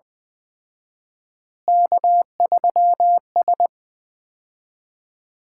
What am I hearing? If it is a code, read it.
Morse code, heard as a keyed tone, about 20 wpm, 700 Hz. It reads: K3S
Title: Common QSO Phrases - 4x Word-Spacing - 20wpm